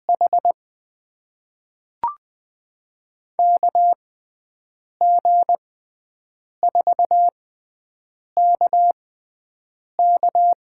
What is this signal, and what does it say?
Morse code, heard as a keyed tone, about 20 wpm, spaced as for 7 wpm, 700 Hz. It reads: H KG4KK
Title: USA Call Signs - 6 Characters - 6x Character Spacing - 20wpm